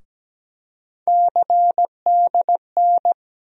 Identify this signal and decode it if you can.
Morse code, heard as a keyed tone, about 17 wpm, 700 Hz. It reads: CDN
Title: USA Call Sign Postfixes - 3 Letters 17wpm